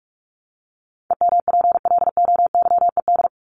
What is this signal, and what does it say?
Morse code, heard as a keyed tone, about 44 wpm, 700 Hz. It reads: EMPLOYED
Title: ED Suffix - Top 100 Words with Base Words 45wpm